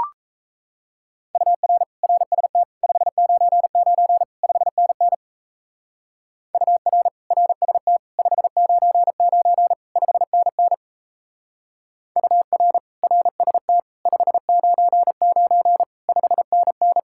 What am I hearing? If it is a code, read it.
Morse code, heard as a keyed tone, about 35 wpm, 700 Hz. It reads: UR RST 599 5NN UR RST 599 5NN UR RST 599 5NN
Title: Speed-Racing Common QSO Phrases - 28wpm